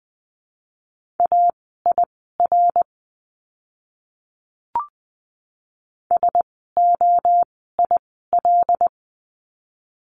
Morse code, heard as a keyed tone, about 20 wpm, spaced as for 14 wpm, 700 Hz. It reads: AIR SOIL
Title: 075 Lesson 13f - Letter L - Words 20wpm